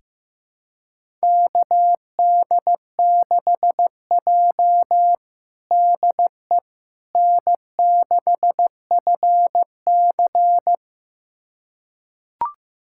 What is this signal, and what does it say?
Morse code, heard as a keyed tone, about 15 wpm, 700 Hz. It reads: KD6J DE N6FC
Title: USA Callsign DE Callsign - 4 Characters - 15wpm